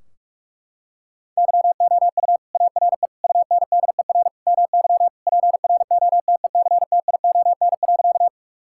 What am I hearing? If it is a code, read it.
Morse code, heard as a keyed tone, about 46 wpm, 700 Hz. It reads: YOU ARE UNDER MY PROTECTION.
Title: Sentences from Top 1600 Words 45wpm